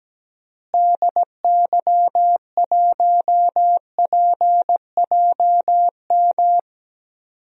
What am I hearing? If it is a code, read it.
Morse code, heard as a keyed tone, about 17 wpm, 700 Hz. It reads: DY1PJM
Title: DXCC-100 Call Signs - 6 Characters - Repeated 3x - 17wpm